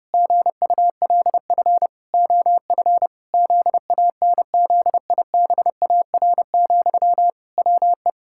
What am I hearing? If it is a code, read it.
Morse code, heard as a keyed tone, about 32 wpm, 700 Hz. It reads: GULF OF ZANZIBAR, WE
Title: Pangram Challenge v2 - 30wpm